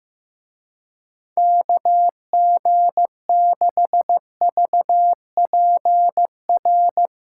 Morse code, heard as a keyed tone, about 15 wpm, 700 Hz. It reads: KG6VPR
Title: Speed Racing - USA 2x3 Call Signs 15wpm